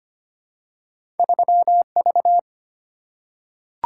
Morse code, heard as a keyed tone, about 25 wpm, 700 Hz. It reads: 3V
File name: Two Letter-Number Combinations - 25wpm